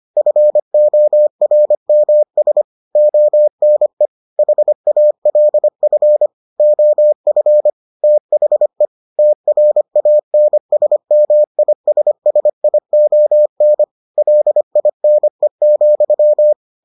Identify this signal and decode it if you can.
Morse code, heard as a keyed tone, about 25 wpm, 600 Hz. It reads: FORMS ONE HALF OF THE TRANSMISSION LINE,